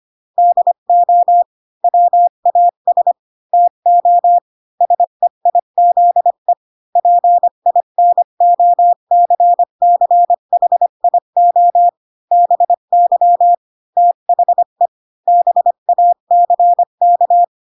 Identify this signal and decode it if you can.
Morse code, heard as a keyed tone, about 25 wpm, 700 Hz. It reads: DO WAS TO SEIZE PINOCCHIO BY THE BACK
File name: Pinocchio / Pinocchio ch3 20 25 ik1hge